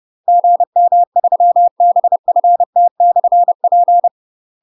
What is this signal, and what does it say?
Morse code, heard as a keyed tone, about 30 wpm, 700 Hz. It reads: GM3BFT/P
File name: UK MASTER QSOs TRAINING / 30WPM / UK MASTER QSOs 0015